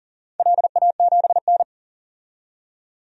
Morse code, heard as a keyed tone, about 40 wpm, 700 Hz. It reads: LA7N
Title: DXCC-100 Call Signs - 4 Characters - 40wpm